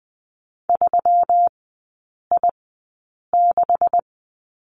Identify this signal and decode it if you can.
Morse code, heard as a keyed tone, about 20 wpm, spaced as for 8 wpm, 700 Hz. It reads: 3I6
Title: 127 Lesson 21e - Number 6 - Three Characters 20wpm